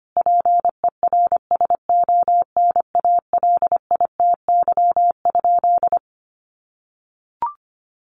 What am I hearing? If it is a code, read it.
Morse code, heard as a keyed tone, about 25 wpm, 700 Hz. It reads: PERSONALITY?
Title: Sentences from Top 1400 Words 25wpm